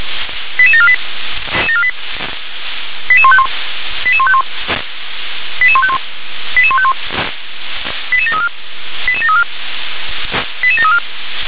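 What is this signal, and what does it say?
Five-tone selective call (ZVEI), heard as quick runs of five tones, tones of about 70 ms. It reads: 88748 88151 88151 88151 88151 88734 88734 88734
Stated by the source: ZVEI five-tone selective call received on SDR.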